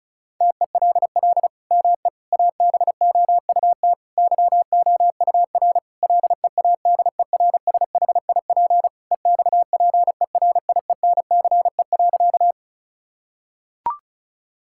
Morse code, heard as a keyed tone, about 36 wpm, 700 Hz. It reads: TELL ME ABOUT YOUR LEADERSHIP EXPERIENCE.
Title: Sentences from Top 1600 Words 35wpm